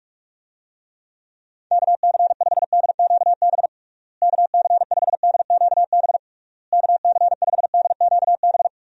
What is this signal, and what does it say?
Morse code, heard as a keyed tone, about 46 wpm, 700 Hz. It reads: KC5DQB KC5DQB KC5DQB
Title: USA Call Signs - 6 Characters - Repeated 3x - 45wpm